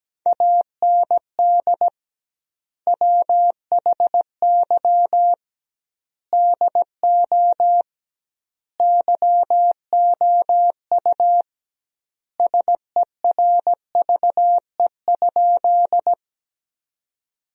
Morse code, heard as a keyed tone, about 17 wpm, 700 Hz. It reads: AND WHY DO YOU SERVE?
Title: 2x Word-Spacing - Sentences from Top 400 Words - 17wpm